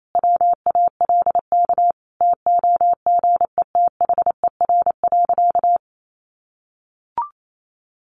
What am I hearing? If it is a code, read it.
Morse code, heard as a keyed tone, about 28 wpm, 700 Hz. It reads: WALK TOGETHER.
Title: Sentences from Top 1400 Words 28wpm